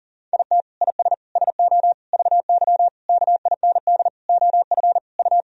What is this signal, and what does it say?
Morse code, heard as a keyed tone, about 40 wpm, 700 Hz. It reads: IT IS SO VY KIND OF U
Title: Sentences from Top 300 Words and Abbreviations 40wpm